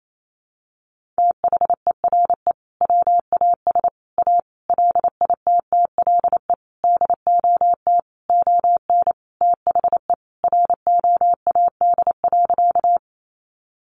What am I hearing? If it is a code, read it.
Morse code, heard as a keyed tone, about 28 wpm, 700 Hz. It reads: THERE WAS A LITTLE DOT ON THE ROAD.
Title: Sentences from Top 1500 Words 28wpm